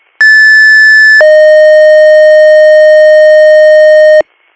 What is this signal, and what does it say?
Two-tone page, heard as a one-second tone, then a three-second tone, about 1750 then 620 Hz. It is heard from 0.2 to 4.2 s.